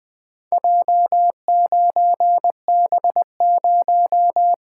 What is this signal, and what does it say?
Morse code, heard as a keyed tone, about 20 wpm, 700 Hz. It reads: J9B0
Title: Four Letter-Number Combinations - 20wpm